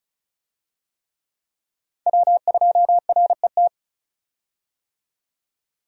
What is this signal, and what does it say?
Morse code, heard as a keyed tone, about 36 wpm, 700 Hz. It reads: W2RET
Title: USA Call Signs - 5 Characters - 35wpm